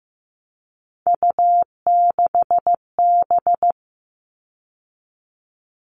Morse code, heard as a keyed tone, about 15 wpm, 700 Hz. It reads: U6B
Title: All Cut Numbers - 3-Digit 15wpm